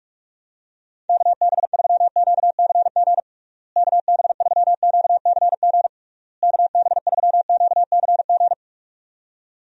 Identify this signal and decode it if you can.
Morse code, heard as a keyed tone, about 42 wpm, 700 Hz. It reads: KB3QCG KB3QCG KB3QCG
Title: USA Call Signs - 6 Characters - Repeated 3x - 45wpm